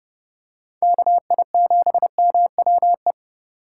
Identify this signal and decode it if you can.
Morse code, heard as a keyed tone, about 31 wpm, 700 Hz. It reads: KI7MWE